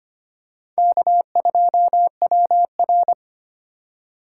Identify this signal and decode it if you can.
Morse code, heard as a keyed tone, about 25 wpm, 700 Hz. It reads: K2WR
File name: practice / 25wpm / Four Letter-Number Combinations - 25wpm